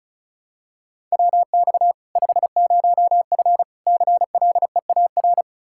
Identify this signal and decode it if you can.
Morse code, heard as a keyed tone, about 34 wpm, 700 Hz. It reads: WX 50F CLEAR
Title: K1USN SST QSO Practice 35wpm